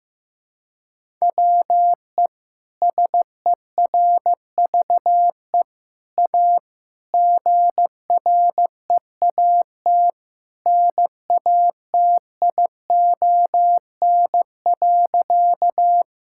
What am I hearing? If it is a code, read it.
Morse code, heard as a keyed tone, about 15 wpm, 700 Hz. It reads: WE SERVE A GREAT NATION.
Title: Sentences from Top 1000 Words - Review Part 6 - 15wpm